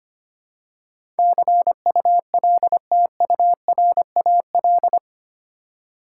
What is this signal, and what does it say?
Morse code, heard as a keyed tone, about 25 wpm, 700 Hz. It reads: CULTURAL
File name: 8 Letter Words - First 3 Letters Given - 25wpm